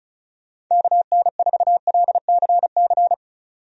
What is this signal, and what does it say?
Morse code, heard as a keyed tone, about 34 wpm, 700 Hz. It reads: KN4LCC
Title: USA Call Signs - 6 Characters - 35wpm